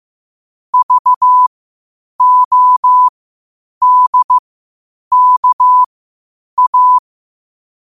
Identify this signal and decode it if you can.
Morse code, heard as a keyed tone, about 15 wpm, spaced as for 8 wpm, 1000 Hz. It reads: VODKA